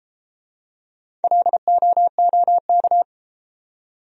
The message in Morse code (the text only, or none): LOOK